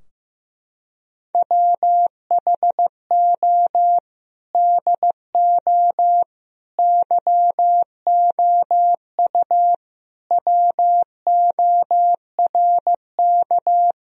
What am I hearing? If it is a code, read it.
Morse code, heard as a keyed tone, about 15 wpm, 700 Hz. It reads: WHO DO YOU WORK
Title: Sentences from Top 1000 Words - Review Part 4 - 15wpm